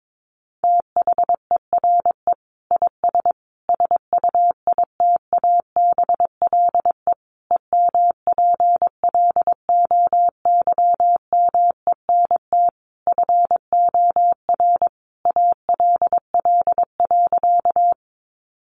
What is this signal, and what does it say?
Morse code, heard as a keyed tone, about 22 wpm, 700 Hz. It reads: THERE IS SUITABLE EMPLOYMENT FOR ALL.
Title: Sentences from Top 1500 Words 22wpm